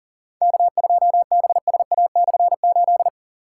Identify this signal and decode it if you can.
Morse code, heard as a keyed tone, about 40 wpm, 700 Hz. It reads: K2BSA/8
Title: K1USN SST QSO Practice - 40wpm